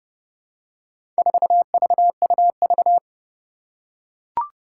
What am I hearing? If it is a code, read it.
Morse code, heard as a keyed tone, about 30 wpm, 690 Hz. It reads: 4VUV
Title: Four Challenging Letter-Number Combinations 30wpm